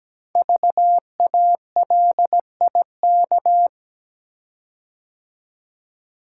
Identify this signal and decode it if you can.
Morse code, heard as a keyed tone, about 17 wpm, 700 Hz. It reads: VALIK
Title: Code Groups - 5 Letters - 17wpm